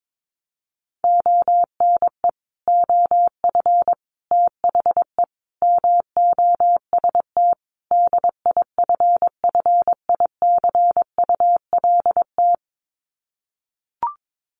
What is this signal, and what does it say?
Morse code, heard as a keyed tone, about 22 wpm, 700 Hz. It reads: ONE OF THE MOST DIFFICULT
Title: Sets of 5 Words Encore2 22wpm